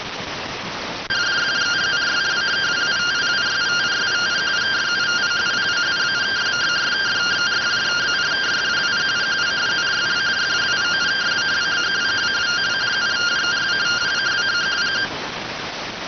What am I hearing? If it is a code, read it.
Two-tone teleprinter signal, about 170 Hz shift.